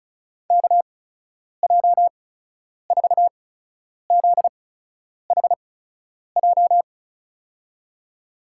Morse code, heard as a keyed tone, about 36 wpm, spaced as for 10 wpm, 700 Hz. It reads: KJ4ZHJ
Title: USA Call Signs - 6 Characters - 8x Character Spacing - 35wpm